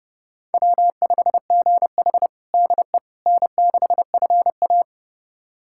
Morse code, heard as a keyed tone, about 30 wpm, 700 Hz. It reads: W5GH DE N6FA
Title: USA Callsign DE Callsign - 4 Characters - 30wpm